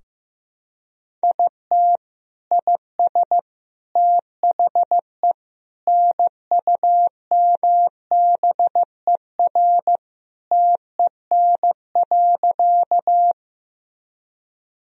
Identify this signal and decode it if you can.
Morse code, heard as a keyed tone, about 15 wpm, 700 Hz. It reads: IT IS THE NUMBER TEN.
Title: Sentences from Top 1000 Words - Review Part 2 - 15wpm